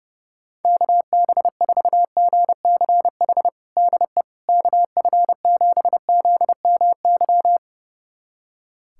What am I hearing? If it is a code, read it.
Morse code, heard as a keyed tone, about 30 wpm, 700 Hz. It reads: KB4GCH DE KF7ZMY
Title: USA Callsign DE Callsign 30wpm